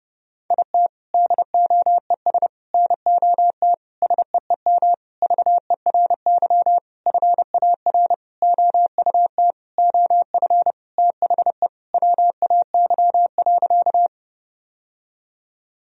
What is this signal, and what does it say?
Morse code, heard as a keyed tone, about 32 wpm, 700 Hz. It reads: IT DOES NOT SEEM VERY FAR OUT OF THE WAY.
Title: Sentences from Top 1000 Words - Review Part 1 - 30wpm